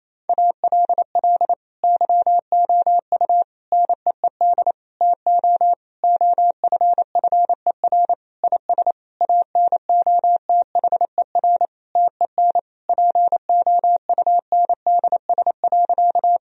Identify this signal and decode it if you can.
Morse code, heard as a keyed tone, about 27 wpm, 700 Hz. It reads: ALL YOU NEED TO OFFER IS ANOTHER TEN POUNDS.
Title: Sentences from Top 1000 Words - Review Part 4 - 28wpm